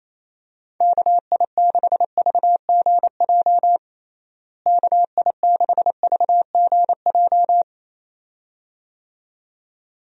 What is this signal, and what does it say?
Morse code, heard as a keyed tone, about 28 wpm, 700 Hz. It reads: KI6VGJ KI6VGJ
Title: USA Call Signs - 6 Characters - Repeated 2x - 28wpm